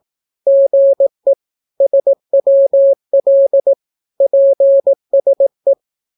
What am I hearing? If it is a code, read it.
Morse code, heard as a keyed tone, about 18 wpm, 550 Hz. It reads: GE SWL PSE
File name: naukapodstaw / skroty / skroty